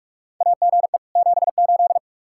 Morse code, heard as a keyed tone, about 46 wpm, 700 Hz. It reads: AGE 78
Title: K1USN SST QSO Practice 45wpm